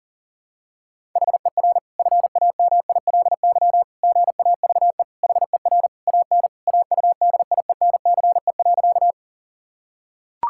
Morse code, heard as a keyed tone, about 37 wpm, 700 Hz. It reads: HER FAMILY GAVE HER AN AUDIENCE.
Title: Sentences from Top 1000 Words - Review Part 5 - 40wpm